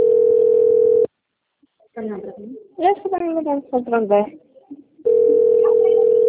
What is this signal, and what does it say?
Ringback tone (North America), 440 + 480 Hz, with speech.
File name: out- - - - - .4407